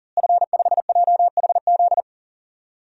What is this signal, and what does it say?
Morse code, heard as a keyed tone, about 40 wpm, 700 Hz. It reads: F5JHZ
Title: DXCC-100 Call Signs - 5 Characters - Repeated 2x - 40wpm